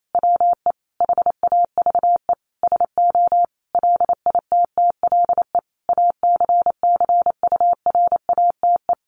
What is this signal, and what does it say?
Morse code, heard as a keyed tone, about 28 wpm, 700 Hz. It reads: WE HAVE SO LITTLE ACCURATE
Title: Sentences from Top 1200 Words 28wpm